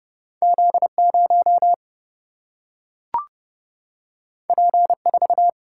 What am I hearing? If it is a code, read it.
Morse code, heard as a keyed tone, about 29 wpm, 700 Hz. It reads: Z0 P4
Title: Single-Letter Single-Number 30wpm